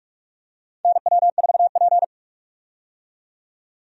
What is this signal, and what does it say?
Morse code, heard as a keyed tone, about 42 wpm, 700 Hz. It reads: NW4P
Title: USA Call Signs - 4 Characters - 45wpm